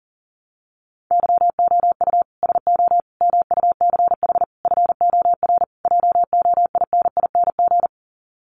Morse code, heard as a keyed tone, about 40 wpm, 700 Hz. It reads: YOU SO MUCH FOR JOINING
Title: Sets of 5 Words Encore2 40wpm